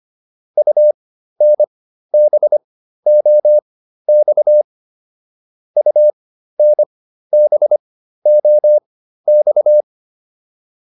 Morse code, heard as a keyed tone, about 25 wpm, spaced as for 13 wpm, 600 Hz. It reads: UNBOX UNBOX